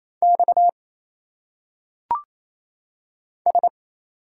Morse code, heard as a keyed tone, about 27 wpm, 700 Hz. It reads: X S
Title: Single Letters - Phonetic Pronunciation 28wpm